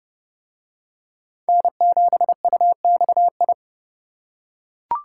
Morse code, heard as a keyed tone, about 29 wpm, 700 Hz. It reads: N7UXI